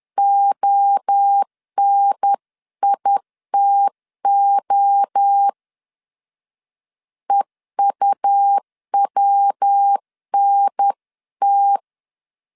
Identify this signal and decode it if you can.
Morse code, heard as a keyed tone, about 10 wpm, 800 Hz. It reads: ONITO EUWNT
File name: 10wpm